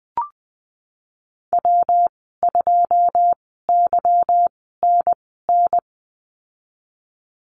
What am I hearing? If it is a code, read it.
Morse code, heard as a keyed tone, about 19 wpm, spaced as for 14 wpm, 700 Hz. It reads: W2YNN
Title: 167 Lesson 27g - Number 7 - Call Signs 20wpm